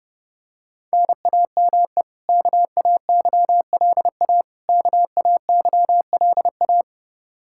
Morse code, heard as a keyed tone, about 32 wpm, 700 Hz. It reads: NAME KAYLA KAYLA
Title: Practice QSO - Exchange 1 - Beginning - 30wpm